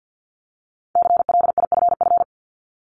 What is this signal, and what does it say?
Morse code, heard as a keyed tone, about 50 wpm, 700 Hz. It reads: CLIFF
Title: Multi-letter Phonemes 50wpm